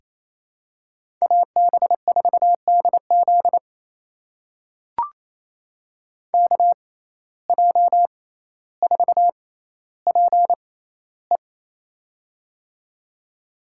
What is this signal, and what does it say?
Morse code, heard as a keyed tone, about 28 wpm, spaced as for 14 wpm, 700 Hz. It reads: AB4DZ KJ4PE
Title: USA Call Signs - 5 Characters - 6x Character Spacing - 28wpm